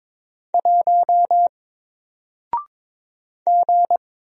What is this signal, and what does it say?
Morse code, heard as a keyed tone, about 22 wpm, 700 Hz. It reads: 1 G
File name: Single-Letter-Number-22wpm